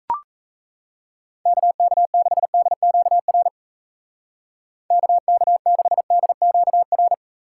Morse code, heard as a keyed tone, about 37 wpm, 700 Hz. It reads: KK6DQR KK6DQR